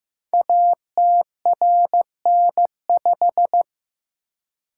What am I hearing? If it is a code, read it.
Morse code, heard as a keyed tone, about 15 wpm, 700 Hz. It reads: ATRN5
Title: Code Groups - 5 Letter:Numbers - 15wpm